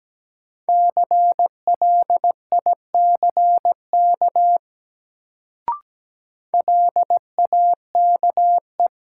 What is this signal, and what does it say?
Morse code, heard as a keyed tone, about 17 wpm, 700 Hz. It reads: CLICK LAKE
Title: Sentences from Top 1100 Words - New Word Drill Alt - 17wpm